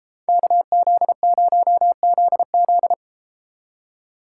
Morse code, heard as a keyed tone, about 31 wpm, 700 Hz. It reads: KZ0ZZ